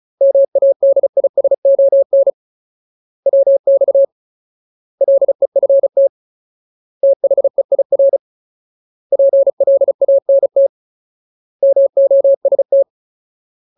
Morse code, heard as a keyed tone, about 34 wpm, 550 Hz. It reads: MADISON WX LEFT THEIR PLANT MOST
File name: Ultimate Word List / 35WPM Ultimate